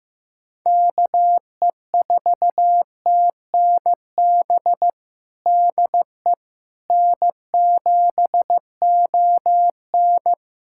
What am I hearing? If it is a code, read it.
Morse code, heard as a keyed tone, about 15 wpm, 700 Hz. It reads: KE4TNB DE N7ON
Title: Practice QSO - Exchange 4 - Ending - 15wpm